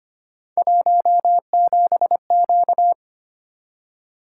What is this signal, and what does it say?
Morse code, heard as a keyed tone, about 25 wpm, 700 Hz. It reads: 17Q
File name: Three Letter-Number Combinations - 25wpm